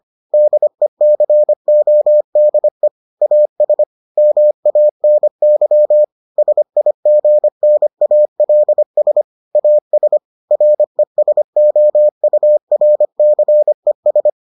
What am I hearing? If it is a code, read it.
Morse code, heard as a keyed tone, about 25 wpm, 600 Hz. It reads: DECODE AS MANY SIGNALS AS RESOURCES